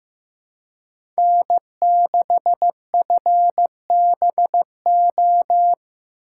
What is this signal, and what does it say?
Morse code, heard as a keyed tone, about 15 wpm, 700 Hz. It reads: N6FBO
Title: USA Call Signs - 5 Characters - 4x Character Spacing - 15wpm